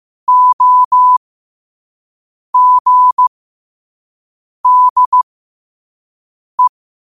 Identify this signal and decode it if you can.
Morse code, heard as a keyed tone, about 15 wpm, spaced as for 5 wpm, 1000 Hz. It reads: OGDE
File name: qotdfiles / 5wpm